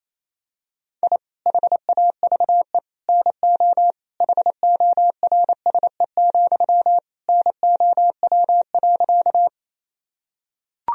Morse code, heard as a keyed tone, about 28 wpm, 700 Hz. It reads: I HAVE NO HORSE, NOW.
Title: Sentences from Top 1000 Words - Review Part 1 - 28wpm